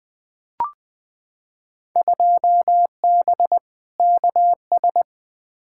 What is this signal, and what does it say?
Morse code, heard as a keyed tone, about 20 wpm, 700 Hz. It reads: 2B KS
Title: Field Day Exchanges Encore - 20wpm